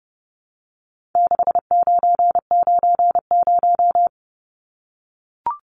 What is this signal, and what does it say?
Morse code, heard as a keyed tone, about 30 wpm, 700 Hz. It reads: 6990
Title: Numbers - 4-Digit 30wpm